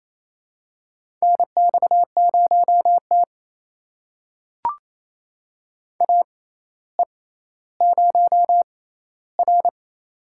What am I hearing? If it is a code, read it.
Morse code, heard as a keyed tone, about 28 wpm, spaced as for 14 wpm, 700 Hz. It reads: NX0T AE0R
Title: USA Call Signs - 4 Characters - 6x Character Spacing - 28wpm